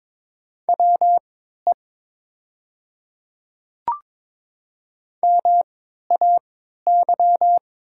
Morse code, heard as a keyed tone, about 22 wpm, spaced as for 13 wpm, 700 Hz. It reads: WE MAY
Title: 3x Character Spacing - Top 200 Words - 22wpm